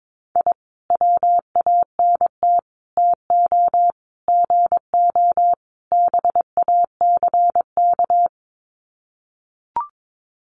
Morse code, heard as a keyed tone, about 22 wpm, 700 Hz. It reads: I WANT TO GO BACK
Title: Sets of 5 Words Encore 22wpm